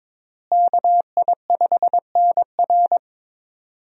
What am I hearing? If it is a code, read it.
Morse code, heard as a keyed tone, about 22 wpm, 700 Hz. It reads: KI5NR
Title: USA Call Signs - 5 Characters - Repeated 2x - 22wpm